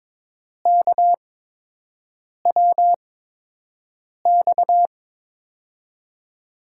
Morse code, heard as a keyed tone, about 22 wpm, spaced as for 6 wpm, 700 Hz. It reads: KWX
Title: USA Call Signs - 6 Characters - 8x Character Spacing - 22wpm